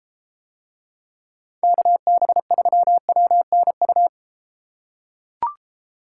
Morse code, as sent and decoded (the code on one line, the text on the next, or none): -.- -... ...-- .-- -. ..-
KB3WNU